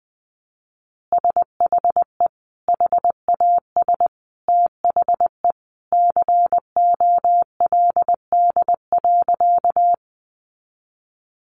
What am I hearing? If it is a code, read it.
Morse code, heard as a keyed tone, about 19 wpm, 700 Hz. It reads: SHE HAS THE COLD.
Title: Easy Phrases 20wpm